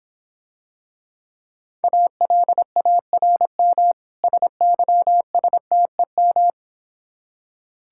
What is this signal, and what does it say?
Morse code, heard as a keyed tone, about 26 wpm, 700 Hz. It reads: ALARM SYSTEM